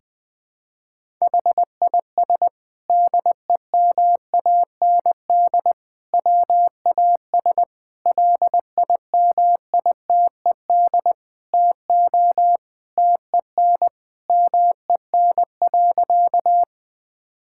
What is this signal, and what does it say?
Morse code, heard as a keyed tone, about 19 wpm, 700 Hz. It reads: HIS DEMAND WAS LIMITED TO TEN MEN.
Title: Sentences from Top 1000 Words - Review Part 6 - 20wpm